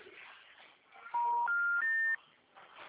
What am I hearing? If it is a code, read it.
Special information tone.